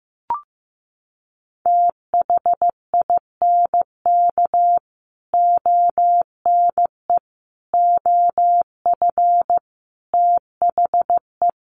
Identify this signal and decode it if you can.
Morse code, heard as a keyed tone, about 15 wpm, 700 Hz. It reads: THINK ONE OF THE
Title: Sets of 5 Words Encore2 15wpm